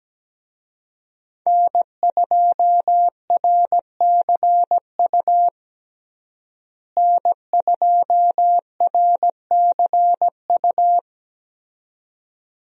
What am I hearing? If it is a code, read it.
Morse code, heard as a keyed tone, about 17 wpm, 700 Hz. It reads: N2RCU N2RCU
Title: USA Call Signs - 5 Characters - Repeated 2x - 17wpm